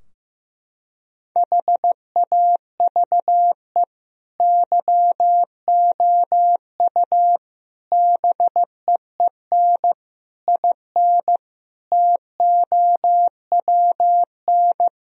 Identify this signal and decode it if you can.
Morse code, heard as a keyed tone, about 15 wpm, 700 Hz. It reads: HAVE YOU BEEN IN TOWN
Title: Sentences from Top 1000 Words - Review Part 2 - 15wpm